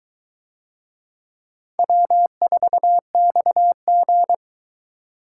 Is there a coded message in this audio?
Yes: Morse code from 1.8 to 4.3 s, heard as a keyed tone, about 21 wpm, 700 Hz. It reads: W4XG